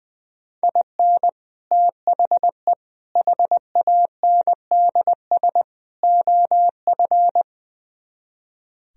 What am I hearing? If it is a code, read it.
Morse code, heard as a keyed tone, about 20 wpm, 700 Hz. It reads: IN THE HANDS OF
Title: Speed-Racing - Sets of 4 Words 20wpm